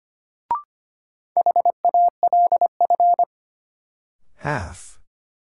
Morse code, heard as a keyed tone, about 25 wpm, 700 Hz. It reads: HALF
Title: Sets of 3 Words - New Word Drill - 25wpm